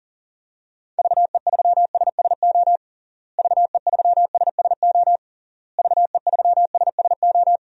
Morse code, heard as a keyed tone, about 40 wpm, 700 Hz. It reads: VE3SSO VE3SSO VE3SSO
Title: DXCC-100 Call Signs - 6 Characters - Repeated 3x - 40wpm